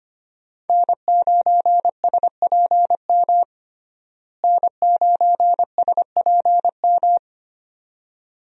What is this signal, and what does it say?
Morse code, heard as a keyed tone, about 25 wpm, 700 Hz. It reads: N9SPM N9SPM
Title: USA Call Signs - 5 Characters - Repeated 2x - 25wpm